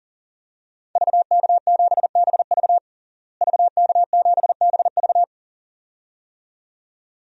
Morse code, heard as a keyed tone, about 37 wpm, 700 Hz. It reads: VK7BV VK7BV
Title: DXCC-100 Call Signs - 5 Characters - Repeated 2x - 40wpm